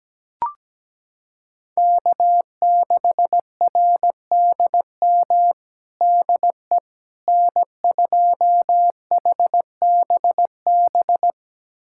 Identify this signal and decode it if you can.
Morse code, heard as a keyed tone, about 17 wpm, 700 Hz. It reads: K6RDM DE N2HBB
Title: USA Callsign DE Callsign - 5 Characters - 17wpm